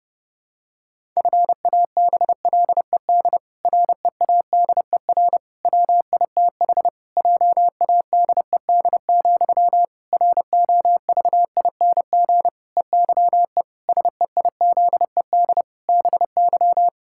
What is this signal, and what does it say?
Morse code, heard as a keyed tone, about 30 wpm, 700 Hz. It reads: FABLED READER WITH JADED, ROVING EYE SEIZED BY
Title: Pangram Challenge v2 - 30wpm